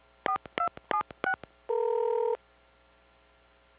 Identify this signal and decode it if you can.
DTMF, heard as telephone touch tones, tones of about 90 ms. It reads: 4276